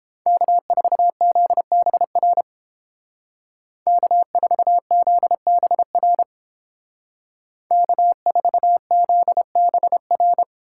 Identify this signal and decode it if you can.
Morse code, heard as a keyed tone, about 29 wpm, 700 Hz. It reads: K4ZBR K4ZBR K4ZBR